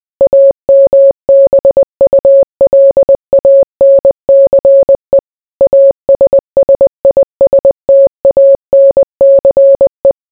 Morse code, heard as a keyed tone, about 20 wpm, 550 Hz. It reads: AMBULANCE ASSISTANCE